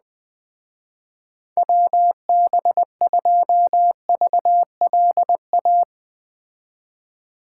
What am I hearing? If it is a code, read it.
Morse code, heard as a keyed tone, about 20 wpm, 700 Hz. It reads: WB2VLA
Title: Speed Racing - USA 2x3 Call Signs 17wpm